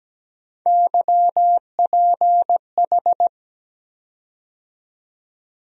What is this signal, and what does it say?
Morse code, heard as a keyed tone, about 17 wpm, 700 Hz. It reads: YPH
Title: USA Call Sign Postfixes - 3 Letters 17wpm